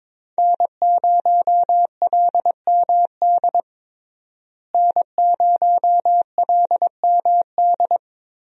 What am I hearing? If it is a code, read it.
Morse code, heard as a keyed tone, about 22 wpm, 700 Hz. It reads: N0LMD N0LMD
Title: USA Call Signs - 5 Characters - Repeated 2x - 22wpm